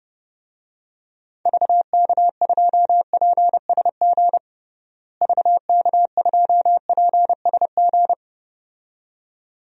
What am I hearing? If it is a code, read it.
Morse code, heard as a keyed tone, about 29 wpm, 700 Hz. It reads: VK2PSG VK2PSG
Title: DXCC-100 Call Signs - 6 Characters - Repeated 2x - 30wpm